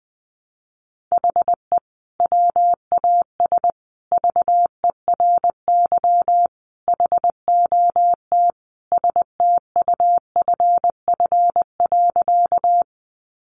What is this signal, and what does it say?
Morse code, heard as a keyed tone, about 20 wpm, 700 Hz. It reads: HE WAS VERY HOT STUFF.
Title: Sentences from Top 1400 Words 20wpm